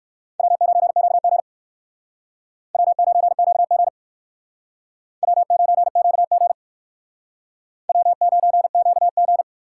Morse code, heard as a keyed tone, about 57 wpm, 700 Hz. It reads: W9QG W9QG W9QG W9QG